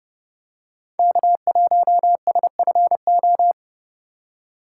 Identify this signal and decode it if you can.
Morse code, heard as a keyed tone, about 30 wpm, 700 Hz. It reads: K1SFO